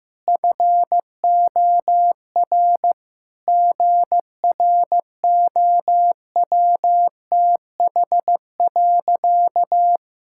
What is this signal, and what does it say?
Morse code, heard as a keyed tone, about 15 wpm, 700 Hz. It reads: FOR GROWTH.
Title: Sentences from Top 1000 Words - Review Part 5 - 15wpm